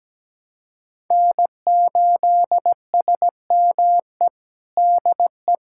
Morse code, heard as a keyed tone, about 17 wpm, 700 Hz. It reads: N8SME DE
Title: USA Callsign DE Callsign - 5 Characters - 17wpm